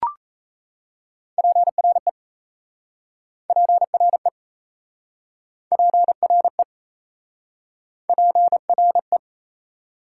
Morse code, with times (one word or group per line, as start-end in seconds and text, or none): 1.4-2.1 PRE
3.5-4.3 PRE
5.7-6.6 PRE
8.1-9.2 PRE